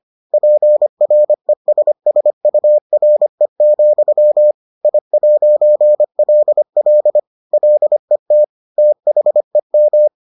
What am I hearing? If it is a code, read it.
Morse code, heard as a keyed tone, about 25 wpm, 600 Hz. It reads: PRESSURE, I'LL LET THEM